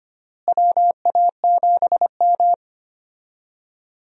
Morse code, heard as a keyed tone, about 25 wpm, 700 Hz. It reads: WA7M
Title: USA Call Signs - 4 Characters - Repeated 2x - 25wpm